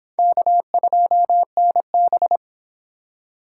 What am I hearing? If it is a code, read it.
Morse code, heard as a keyed tone, about 22 wpm, 700 Hz. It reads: K2NB